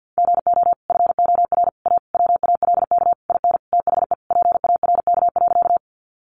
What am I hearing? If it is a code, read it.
Morse code, heard as a keyed tone, about 48 wpm, 700 Hz. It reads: GO FOR A WALK IN THE PARK.
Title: Sentences from Top 1300 Words 50wpm